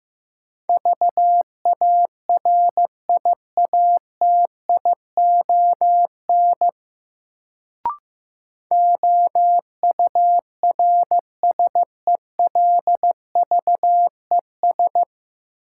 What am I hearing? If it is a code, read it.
Morse code, heard as a keyed tone, about 15 wpm, 700 Hz. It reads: VARIATION OURSELVES
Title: Sentences from Top 1600 Words - New Word Drill Alt - 15wpm